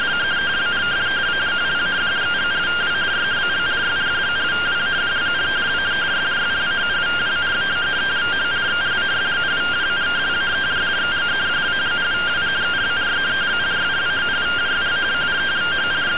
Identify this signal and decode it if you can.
Two-tone teleprinter signal, about 170 Hz shift.